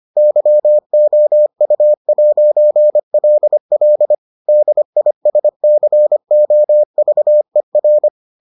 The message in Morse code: YOU'LL DISCOVER

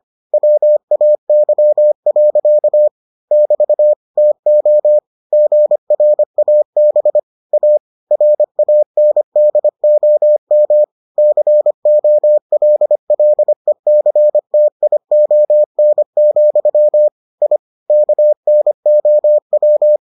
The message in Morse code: WAY. <BT> TO GRAB A RANDOM COLLECTION, I KNOW

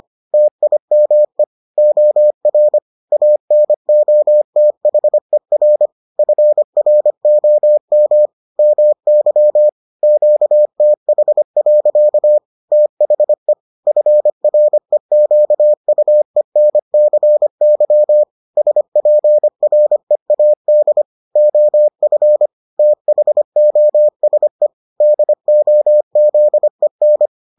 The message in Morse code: TIME OR ANOTHER FROM MY QTH. THE FREQUENCY SPREAD OF THOSE DOZEN

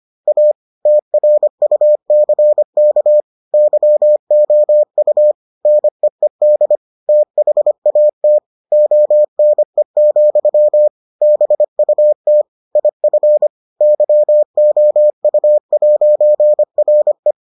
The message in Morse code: A TRUCK YOU NEED THAT ONE, BUT IF YOU'RE

